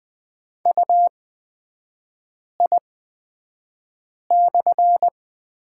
Morse code, heard as a keyed tone, about 20 wpm, spaced as for 5 wpm, 700 Hz. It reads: UI/
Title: 243 Lesson 39e - Letter z - Three Characters 20wpm